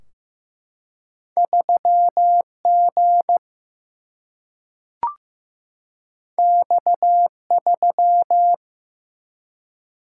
Morse code, heard as a keyed tone, about 15 wpm, 700 Hz. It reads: 3G X3
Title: Single-Letter Single-Number 15wpm